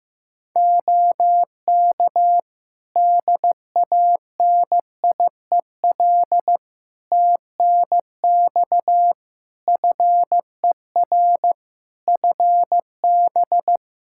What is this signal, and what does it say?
Morse code, heard as a keyed tone, about 15 wpm, 700 Hz. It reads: OK DANIEL TNX FER FB
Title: Practice QSO - Exchange 4 - Ending - 15wpm